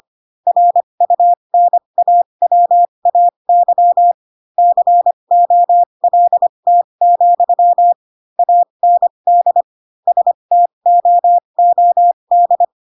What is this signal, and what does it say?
Morse code, heard as a keyed tone, about 25 wpm, spaced as for 22 wpm, 700 Hz. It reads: RUNAWAY COLT, AND STOOD
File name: Pinocchio / Pinocchio ch3 18 25 ik1hge